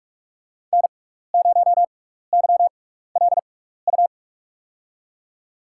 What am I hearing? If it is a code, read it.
Morse code, heard as a keyed tone, about 44 wpm, spaced as for 15 wpm, 690 Hz. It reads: N0YLU
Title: USA Call Signs - 5 Characters - 6x Character Spacing - 45wpm